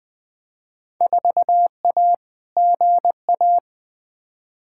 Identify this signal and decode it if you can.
Morse code, heard as a keyed tone, about 20 wpm, 700 Hz. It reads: 4A GA
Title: Field Day Exchanges - 20wpm